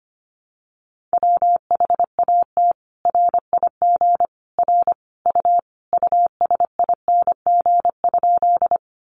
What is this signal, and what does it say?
Morse code, heard as a keyed tone, about 25 wpm, 700 Hz. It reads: WHAT RIG R U USING?
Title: Common QSO Phrases 25wpm